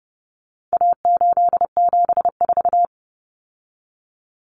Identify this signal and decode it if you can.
Morse code, heard as a keyed tone, about 30 wpm, 700 Hz. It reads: A874
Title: Common Cut Numbers - 4-Digit 30wpm